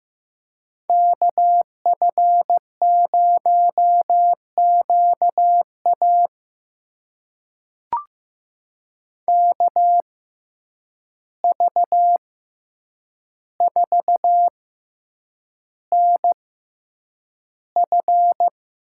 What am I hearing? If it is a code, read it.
Morse code, heard as a keyed tone, about 15 wpm, spaced as for 8 wpm, 700 Hz. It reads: KF0QA KV4NF
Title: USA Call Signs - 5 Characters - 6x Character Spacing - 15wpm